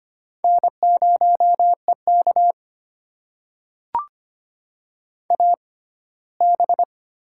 Morse code, heard as a keyed tone, about 25 wpm, spaced as for 17 wpm, 700 Hz. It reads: N0EK AB
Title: USA Call Signs - 6x Character Spacing - 25wpm